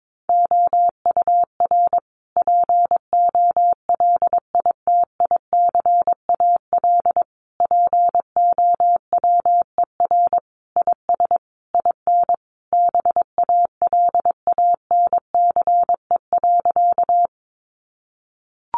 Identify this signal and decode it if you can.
Morse code, heard as a keyed tone, about 22 wpm, 700 Hz. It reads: OUR POLITICAL POWER IS IN BALANCE.
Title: Sentences from Top 900 Words 22wpm